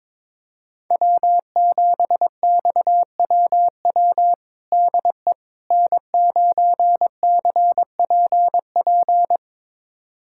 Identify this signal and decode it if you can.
Morse code, heard as a keyed tone, about 22 wpm, 700 Hz. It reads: W7XWW DE N9CPP
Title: USA Callsign DE Callsign - 5 Characters - 22wpm